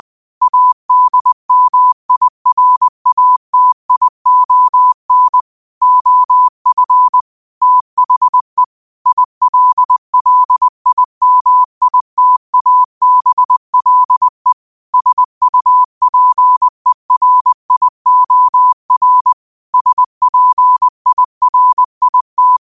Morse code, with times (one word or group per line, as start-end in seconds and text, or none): 0.4-5.4 ADMIRATION
5.8-7.2 OF
7.6-8.6 THE
9.1-14.5 ILLIMITABLE
14.9-19.3 SUPERIOR
19.7-22.6 SPIRIT